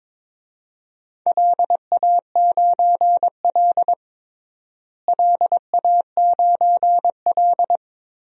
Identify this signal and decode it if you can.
Morse code, heard as a keyed tone, about 22 wpm, 700 Hz. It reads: LA9L LA9L
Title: DXCC-100 Call Signs - 4 Characters - Repeated 2x - 22wpm